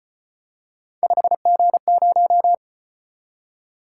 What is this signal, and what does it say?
Morse code, heard as a keyed tone, about 33 wpm, 700 Hz. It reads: 5G0